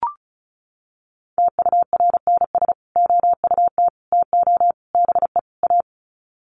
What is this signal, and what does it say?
Morse code, heard as a keyed tone, about 34 wpm, 700 Hz. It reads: TURNS OUT TO BE A
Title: Sets of 5 Words Encore 35wpm